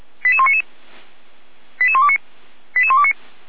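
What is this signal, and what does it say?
Five-tone selective call (ZVEI), heard as quick runs of five tones, tones of about 70 ms. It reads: 88199 88129 88129
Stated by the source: ZVEI five-tone selective call received by SDR.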